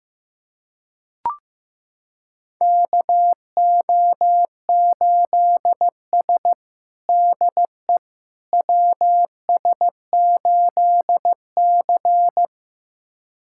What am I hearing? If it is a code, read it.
Morse code, heard as a keyed tone, about 15 wpm, 700 Hz. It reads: KO8S DE WS8C
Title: USA Callsign DE Callsign - 4 Characters - 15wpm